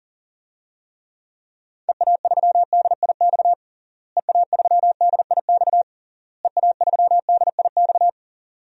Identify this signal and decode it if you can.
Morse code, heard as a keyed tone, about 37 wpm, 700 Hz. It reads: EA3DIX EA3DIX EA3DIX
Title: DXCC-100 Call Signs - 6 Characters - Repeated 3x - 40wpm